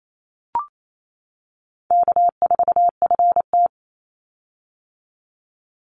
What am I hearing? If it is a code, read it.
Morse code, heard as a keyed tone, about 28 wpm, 700 Hz. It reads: K4FT
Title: Easy Call Signs - 28wpm